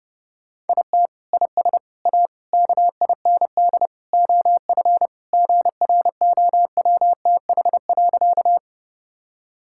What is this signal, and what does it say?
Morse code, heard as a keyed tone, about 32 wpm, 700 Hz. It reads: IT IS A KIND OF GROWTH.
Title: Sentences from Top 1000 Words - Review Part 5 - 30wpm